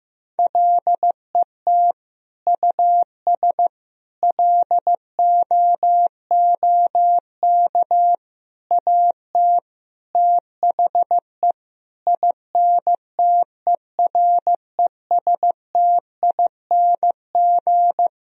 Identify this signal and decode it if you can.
Morse code, heard as a keyed tone, about 16 wpm, 700 Hz. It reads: LET US LOOK AT THE INTERESTING
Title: Sentences from Top 1000 Words - Review Part 2 - 15wpm